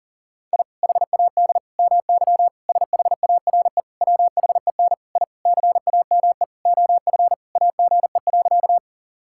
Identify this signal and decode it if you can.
Morse code, heard as a keyed tone, about 37 wpm, 700 Hz. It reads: I HAD MY SHARE WHEN I CAME OF AGE.
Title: Sentences from Top 1000 Words - Review Part 6 - 40wpm